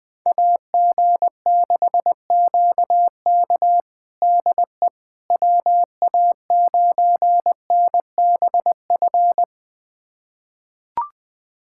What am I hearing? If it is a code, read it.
Morse code, heard as a keyed tone, about 20 wpm, 700 Hz. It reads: AG6QK DE WA9NBF
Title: USA Callsign DE Callsign 20wpm